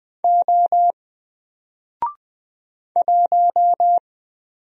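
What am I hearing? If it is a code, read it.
Morse code, heard as a keyed tone, about 20 wpm, 700 Hz. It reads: O 1
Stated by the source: Single letter and number Morse drill at 20 wpm, from N4LCD.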